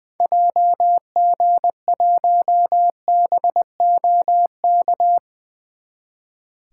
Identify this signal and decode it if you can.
Morse code, heard as a keyed tone, about 20 wpm, 700 Hz. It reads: JG1BOK
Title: 2023 SOTA Activator Call Signs - 20wpm